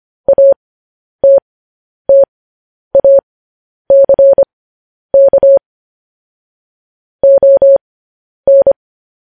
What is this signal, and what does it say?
Morse code, heard as a keyed tone, about 25 wpm, spaced as for 10 wpm, 550 Hz. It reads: ATTACK ON
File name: podcast / News headlines-1x-25x10